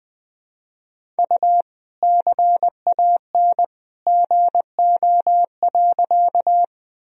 Morse code, heard as a keyed tone, about 20 wpm, 700 Hz. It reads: U CAN GO.
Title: Sentences from Top 100 Words and Abbreviations 20wpm